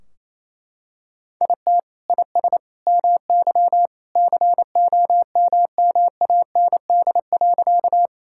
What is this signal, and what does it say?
Morse code, heard as a keyed tone, about 27 wpm, 700 Hz. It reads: IT IS MY COMMAND.
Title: Sentences from Top 1600 Words 28wpm